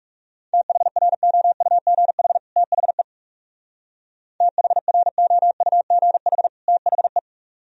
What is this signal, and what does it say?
Morse code, heard as a keyed tone, about 37 wpm, 700 Hz. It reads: THROUGH THE THROUGH THE
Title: Speed-Racing - Sets of 2 Words 30wpm